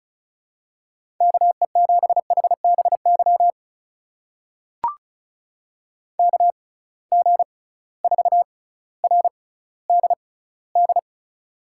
Morse code, heard as a keyed tone, about 36 wpm, spaced as for 18 wpm, 700 Hz. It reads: KE7HBY KG4RDD
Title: USA Call Signs - 6 Characters - 6x Character Spacing - 35wpm